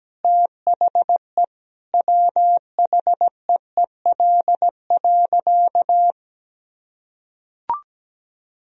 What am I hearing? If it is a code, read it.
Morse code, heard as a keyed tone, about 17 wpm, 700 Hz. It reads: THE WHEEL.
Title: Sentences from Top 1000 Words - Review Part 3 - 17wpm